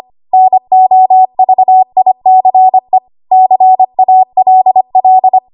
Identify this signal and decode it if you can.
Morse code, heard as a keyed tone, about 25 wpm, 750 Hz. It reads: NOVICE CALL
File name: Archive / 25 WPM / 25WPM